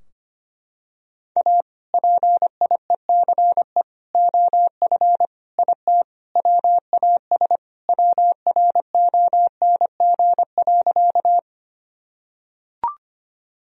Morse code, heard as a keyed tone, about 25 wpm, 700 Hz. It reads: A PIECE OF IT WAS WRONG.
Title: Sentences from Top 1000 Words - Review Part 2 - 25wpm